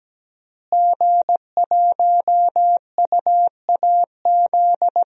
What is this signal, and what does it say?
Morse code, heard as a keyed tone, about 17 wpm, 700 Hz. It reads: G1UAZ
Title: DXCC-100 Call Signs - 5 Characters - Repeated 2x - 17wpm